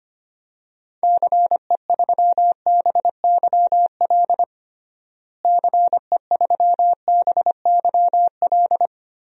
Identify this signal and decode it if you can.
Morse code, heard as a keyed tone, about 25 wpm, 700 Hz. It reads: CE3BYL CE3BYL
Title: DXCC-100 Call Signs - 6 Characters - Repeated 3x - 25wpm